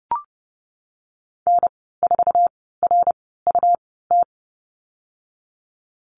Morse code, heard as a keyed tone, about 30 wpm, 700 Hz. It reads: N 4 R U T
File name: Lesson 19g - Letter w - Call Signs 30wpm